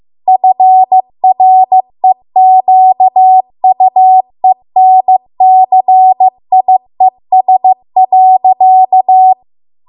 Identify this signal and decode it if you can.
Morse code, heard as a keyed tone, about 15 wpm, 750 Hz. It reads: FREQUENCIES.